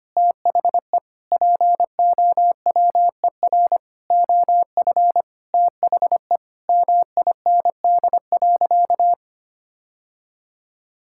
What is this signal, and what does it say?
Morse code, heard as a keyed tone, about 25 wpm, 700 Hz. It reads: THE POWER OF THE MIND.
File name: practice / 25wpm / Sentences with 4-Word Phrases - 25wpm